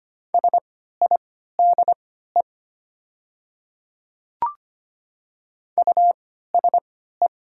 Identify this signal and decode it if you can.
Morse code, heard as a keyed tone, about 25 wpm, spaced as for 14 wpm, 700 Hz. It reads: SIDE USE
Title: 3x Character Spacing - Top 100 Words - 25wpm